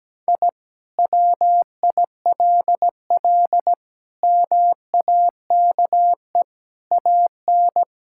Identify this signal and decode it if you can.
Morse code, heard as a keyed tone, about 17 wpm, 700 Hz. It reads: I WILL MAKE AN
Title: Sentences from Top 1000 Words - Review Part 5 - 17wpm